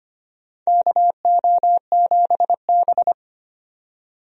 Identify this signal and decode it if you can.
Morse code, heard as a keyed tone, about 25 wpm, 700 Hz. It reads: KO7B
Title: USA Call Signs - 4 Characters - 25wpm